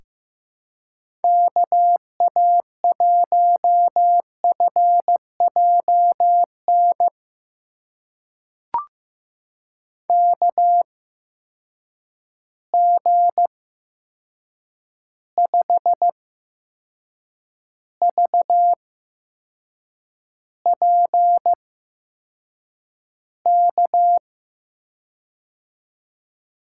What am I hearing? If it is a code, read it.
Morse code, heard as a keyed tone, about 15 wpm, spaced as for 6 wpm, 700 Hz. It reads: KA1FJN KG5VPK
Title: USA Call Signs - 6 Characters - 8x Character Spacing - 15wpm